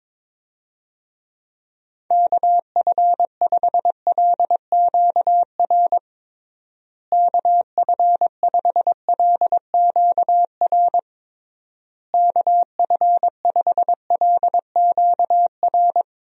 Morse code, heard as a keyed tone, about 22 wpm, 700 Hz. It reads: KF5LQR KF5LQR KF5LQR
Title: USA Call Signs - 6 Characters - Repeated 3x - 22wpm